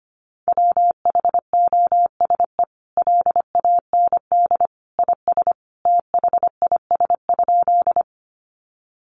Morse code, heard as a keyed tone, about 25 wpm, 700 Hz. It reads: WHOSE LAND IS THIS?
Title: Sentences from Top 1200 Words 25wpm